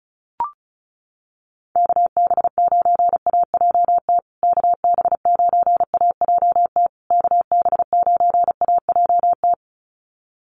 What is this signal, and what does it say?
Morse code, heard as a keyed tone, about 34 wpm, 700 Hz. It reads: KB9AJT KB9AJT KB9AJT
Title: USA Call Signs Repeated 3x 35wpm